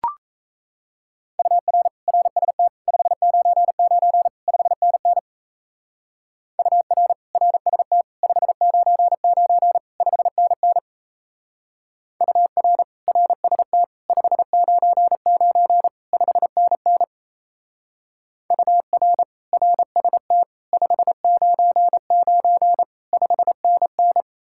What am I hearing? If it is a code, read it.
Morse code, heard as a keyed tone, about 36 wpm, 700 Hz. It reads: UR RST 599 5NN UR RST 599 5NN UR RST 599 5NN UR RST 599 5NN